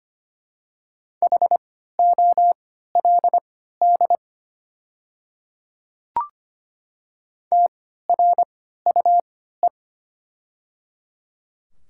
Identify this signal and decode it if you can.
Morse code, heard as a keyed tone, about 25 wpm, spaced as for 14 wpm, 700 Hz. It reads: HOLD TRUE
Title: 3x Character Spacing - Top 500 Words - 25wpm